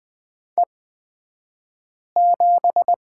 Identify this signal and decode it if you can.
Morse code, heard as a keyed tone, about 20 wpm, spaced as for 5 wpm, 700 Hz. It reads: E7
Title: 242 Lesson 39d - Letter z - Two Characters 20wpm